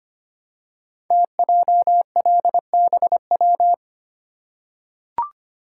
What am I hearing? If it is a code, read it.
Morse code, heard as a keyed tone, about 25 wpm, 700 Hz. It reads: TJLBW
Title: Code Groups - 5 Letter:Numbers - 25wpm